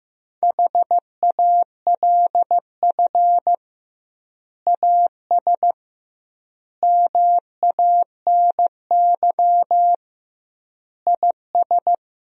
Morse code, heard as a keyed tone, about 15 wpm, 700 Hz. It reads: HALF AS MANY IS
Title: 2x Word-Spacing - Sentences from Top 400 Words - 15wpm